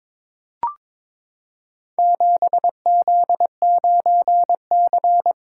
Morse code, heard as a keyed tone, about 22 wpm, 700 Hz. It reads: 7Z9C
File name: practice / 22wpm / Four Letter-Number Combinations - 22wpm